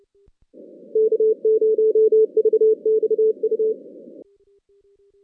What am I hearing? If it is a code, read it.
Morse code, heard as a keyed tone, about 28 wpm, 410 Hz. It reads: K0VXU
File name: 2016 JAN CW NAQP / 2016 Jan naqp k5ot - 0-8